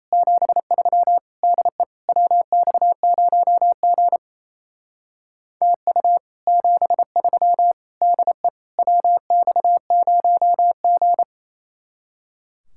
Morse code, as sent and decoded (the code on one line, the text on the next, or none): --... ...-- / -.. / .-- -..- ----- --. / - ..- / --... ...-- / -.. / .-- -..- ----- --.
73 DE WX0G TU 73 DE WX0G